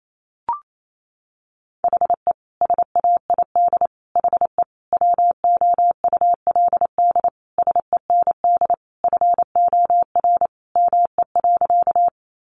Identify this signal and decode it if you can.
Morse code, heard as a keyed tone, about 28 wpm, 700 Hz. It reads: HE SAID HE WOULD SEND FOR ME.
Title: Sentences from Top 1500 Words 28wpm